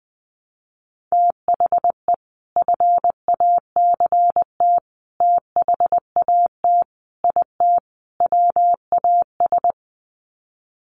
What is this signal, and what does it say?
Morse code, heard as a keyed tone, about 20 wpm, 700 Hz. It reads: THE FACT THAT IT WAS
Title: Sets of 5 Words Encore2 20wpm